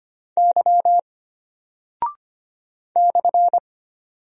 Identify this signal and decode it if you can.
Morse code, heard as a keyed tone, about 25 wpm, 700 Hz. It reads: Y /
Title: Single Character (Letter:Number:Punctuation:Prosign) Alt 25wpm